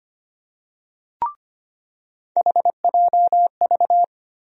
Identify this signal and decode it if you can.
Morse code, heard as a keyed tone, about 25 wpm, 700 Hz. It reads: HJV
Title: USA Call Sign Postfixes - 3 Letters 25wpm